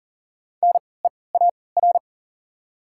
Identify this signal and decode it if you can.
Morse code, heard as a keyed tone, about 40 wpm, spaced as for 23 wpm, 700 Hz. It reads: NEAR
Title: 3x Character Spacing - Top 200 Words - 40wpm